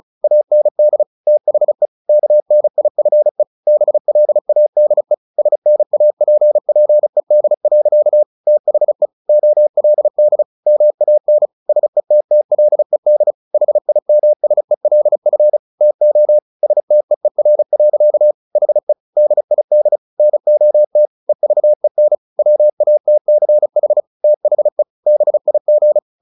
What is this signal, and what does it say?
Morse code, heard as a keyed tone, about 34 wpm, 600 Hz. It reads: AND THE KNIFE BLADE SNAPPED. THE OLD MAN SETTLED HIMSELF TO STEER. HE DID NOT EVEN WATCH THE BIG
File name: the old man and the sea 2 35wpm